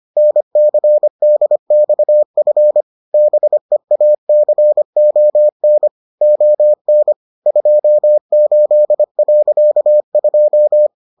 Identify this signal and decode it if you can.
Morse code, heard as a keyed tone, about 25 wpm, 600 Hz. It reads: NCDXF BEACON ON 28.2